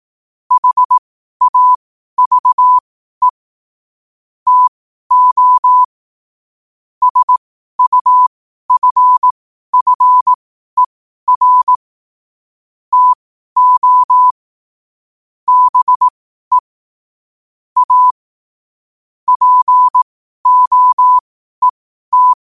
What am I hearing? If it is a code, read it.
Morse code, heard as a keyed tone, about 18 wpm, spaced as for 12 wpm, 1000 Hz. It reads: HAVE TO SUFFER TO BE A POET
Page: New quotes every day in morse code at 12 Words per minute.